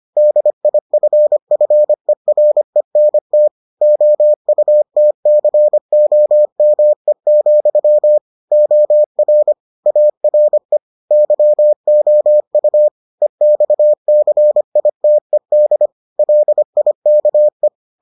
Morse code, heard as a keyed tone, about 25 wpm, 600 Hz. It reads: DIFFERENT OUTCOME, OR ARE YOU EXCITED LIKE